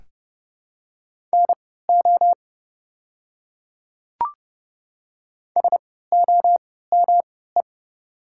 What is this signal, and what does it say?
Morse code, heard as a keyed tone, about 32 wpm, spaced as for 17 wpm, 700 Hz. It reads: NO SOME
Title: 3x Character Spacing - Top 100 Words - 30wpm